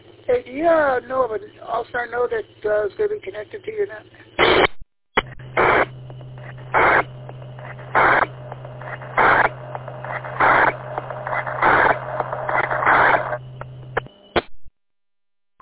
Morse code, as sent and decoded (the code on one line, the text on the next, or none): none